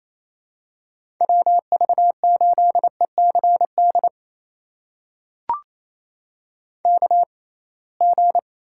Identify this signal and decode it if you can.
Morse code, heard as a keyed tone, about 28 wpm, spaced as for 21 wpm, 700 Hz. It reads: WV8ECD KG
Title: USA Call Signs - 6 Characters - 6x Character Spacing - 28wpm